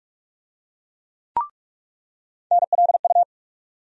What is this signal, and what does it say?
Morse code, heard as a keyed tone, about 46 wpm, 700 Hz. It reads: NLU
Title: USA Call Sign Postfixes - 3 Letters 45wpm